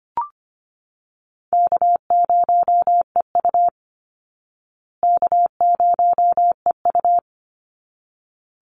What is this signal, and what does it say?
Morse code, heard as a keyed tone, about 25 wpm, 700 Hz. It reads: K0EU K0EU
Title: Call Signs Repeated 3x - 25wpm